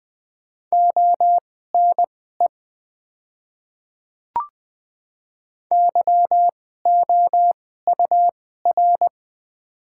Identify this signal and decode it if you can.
Morse code, heard as a keyed tone, about 20 wpm, spaced as for 15 wpm, 700 Hz. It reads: ONE YOUR
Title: 2x Character Spacing - Top 100 Words - 20wpm